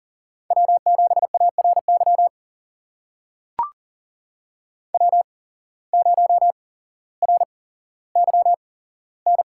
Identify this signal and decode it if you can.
Morse code, heard as a keyed tone, about 39 wpm, spaced as for 17 wpm, 700 Hz. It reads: W7ARY W0RYN
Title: USA Call Signs - 5 Characters - 8x Character Spacing - 40wpm